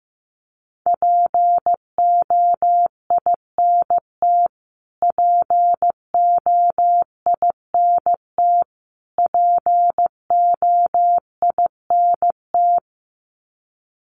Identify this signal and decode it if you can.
Morse code, heard as a keyed tone, about 15 wpm, 700 Hz. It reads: POINT POINT POINT
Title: Top 200 Words Repeated 15wpm